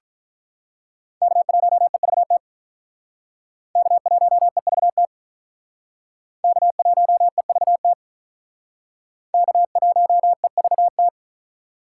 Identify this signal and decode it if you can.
Morse code, heard as a keyed tone, about 43 wpm, 700 Hz. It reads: K1EVT K1EVT K1EVT K1EVT
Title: Speed Racing - USA 1x3 Call Signs 35wpm